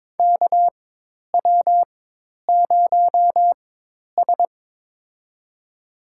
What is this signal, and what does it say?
Morse code, heard as a keyed tone, about 22 wpm, spaced as for 10 wpm, 700 Hz. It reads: KW0S
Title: USA Call Signs - 4x Character Spacing - 22wpm